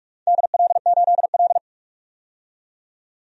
Morse code, heard as a keyed tone, about 44 wpm, 700 Hz. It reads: DL8L
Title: DXCC-100 Call Signs - 4 Characters - 45wpm